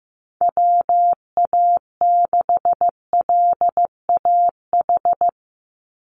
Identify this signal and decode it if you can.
Morse code, heard as a keyed tone, about 15 wpm, 700 Hz. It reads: WA6LAH
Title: USA Call Signs 15wpm